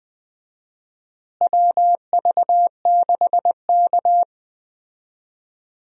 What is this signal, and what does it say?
Morse code, heard as a keyed tone, about 20 wpm, 700 Hz. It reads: WV6K
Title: USA Call Signs - 4 Characters - Repeated 2x - 20wpm